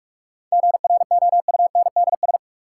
Morse code, heard as a keyed tone, about 46 wpm, 700 Hz. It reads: GROUNDS
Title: Sentences from Top 1300 Words - New Word Drill Alt - 45wpm